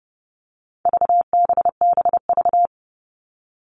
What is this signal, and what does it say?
Morse code, heard as a keyed tone, about 30 wpm, 700 Hz. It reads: VBBV
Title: All Cut Numbers - 4-Digit 30wpm